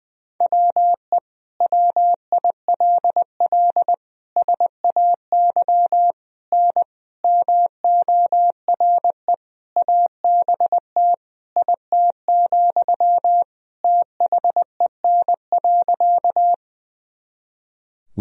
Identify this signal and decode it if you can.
Morse code, heard as a keyed tone, about 21 wpm, 700 Hz. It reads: WE WILL SAY N MORE ABT IT, THEN.
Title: Sentences from Top 200 Words and Abbreviations 20wpm